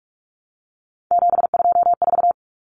Morse code, heard as a keyed tone, about 44 wpm, 700 Hz. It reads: 724
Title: Numbers - 3-Digit 45wpm